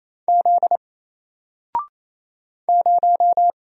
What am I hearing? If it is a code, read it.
Morse code, heard as a keyed tone, about 28 wpm, 700 Hz. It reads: Z 0
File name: Single-Letter-Number-28wpm